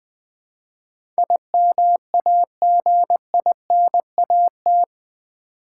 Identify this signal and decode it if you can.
Morse code, heard as a keyed tone, about 20 wpm, 700 Hz. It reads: IMAGINAT
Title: Build A Word - 4 Letters At A Time - 11-Letter Words - 20wpm